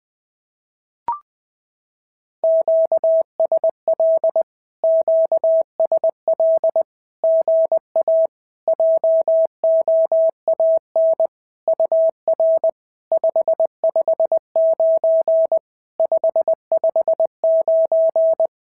Morse code, heard as a keyed tone, about 20 wpm, 650 Hz. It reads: QSL QSL GA JOAN UR 559 559
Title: SOTA QSO Practice 20wpm